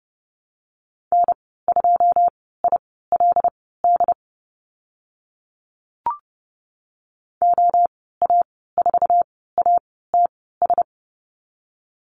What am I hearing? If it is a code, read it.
Morse code, heard as a keyed tone, about 30 wpm, spaced as for 17 wpm, 700 Hz. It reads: N2ILD OA4ATS
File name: Lesson 14g - Number 2 - Call Signs 30wpm